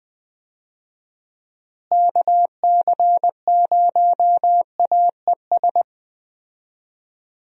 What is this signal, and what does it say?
Morse code, heard as a keyed tone, about 20 wpm, 700 Hz. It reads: KC0AES
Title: USA Call Signs - 6 Characters - Repeated 3x - 20wpm